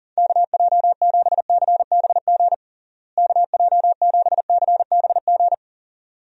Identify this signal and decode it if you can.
Morse code, heard as a keyed tone, about 40 wpm, 700 Hz. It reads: KJ7CBG KJ7CBG
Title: USA Call Signs - 6 Characters - Repeated 2x - 40wpm